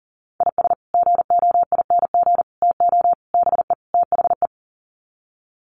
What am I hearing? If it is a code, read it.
Morse code, heard as a keyed tone, about 40 wpm, 700 Hz. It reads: IS GOING TO BE THE
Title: Sets of 5 Words 40wpm